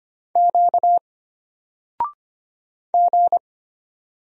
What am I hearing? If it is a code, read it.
Morse code, heard as a keyed tone, about 25 wpm, 700 Hz. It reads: Q G
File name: Single-Letter-Number-25wpm